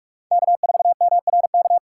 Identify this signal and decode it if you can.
Morse code, heard as a keyed tone, about 46 wpm, 700 Hz. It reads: K4MRK
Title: USA Call Signs - 5 Characters - 3x Character Spacing - 45wpm